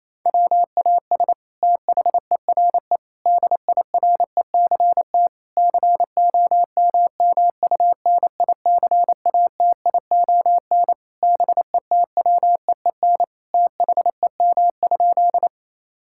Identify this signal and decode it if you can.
Morse code, heard as a keyed tone, about 27 wpm, 700 Hz. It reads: WAS THERE DIRECT COMMUNICATION BETWEEN THEM?
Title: Sentences from Top 1000 Words - Review Part 6 - 28wpm